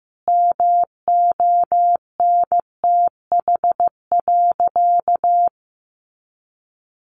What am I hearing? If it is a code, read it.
Morse code, heard as a keyed tone, about 15 wpm, 700 Hz. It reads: MONTH.
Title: Easy Phrases 15wpm